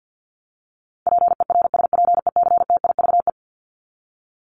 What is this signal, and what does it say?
Morse code, heard as a keyed tone, about 50 wpm, 700 Hz. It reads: PERSPECTIVE
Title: Top 1001-1250 Words 50wpm